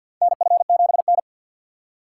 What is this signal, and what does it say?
Morse code, heard as a keyed tone, about 45 wpm, 700 Hz. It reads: NF7N
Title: USA Call Signs - 4 Characters - 2x Character Spacing - 50wpm